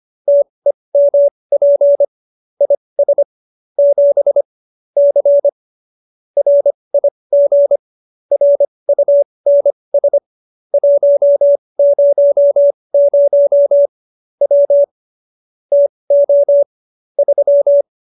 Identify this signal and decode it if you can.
Morse code, heard as a keyed tone, about 25 wpm, spaced as for 20 wpm, 570 Hz. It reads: TEMP IS 7 C RIG RUNS 100 W TO 3